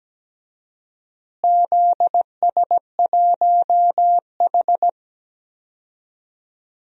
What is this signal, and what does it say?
Morse code, heard as a keyed tone, about 17 wpm, 700 Hz. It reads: ZS1H
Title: DXCC-100 Call Signs - 4 Characters - Repeated 3x - 17wpm